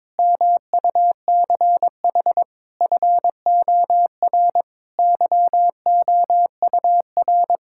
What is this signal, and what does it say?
Morse code, heard as a keyed tone, about 22 wpm, 700 Hz. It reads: MUCH FOR YOUR
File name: practice / 22wpm / Sentences with 4-Word Phrases - 22wpm